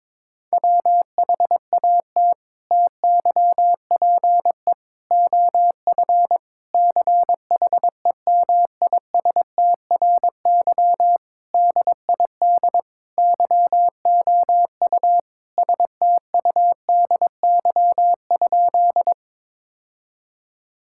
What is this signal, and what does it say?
Morse code, heard as a keyed tone, about 22 wpm, 700 Hz. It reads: WHAT TYPE OF CHEMISTRY DID YOU STUDY?
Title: Sentences from Top 1000 Words - Review Part 6 - 22wpm